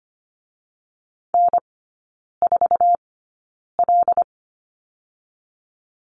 Morse code, heard as a keyed tone, about 25 wpm, spaced as for 9 wpm, 700 Hz. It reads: N4L
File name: Lesson 17e - Letter u - Three Characters 25wpm